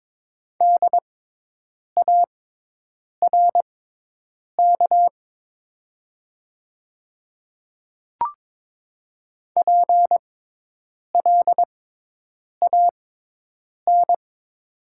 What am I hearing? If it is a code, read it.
Morse code, heard as a keyed tone, about 22 wpm, spaced as for 8 wpm, 700 Hz. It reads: DARK PLAN
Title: 6x Character Spacing - Top 500 Words - 22wpm